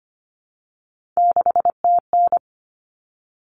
Morse code, heard as a keyed tone, about 25 wpm, 700 Hz. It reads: 6TN
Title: Common Cut Numbers - 3-Digit 25wpm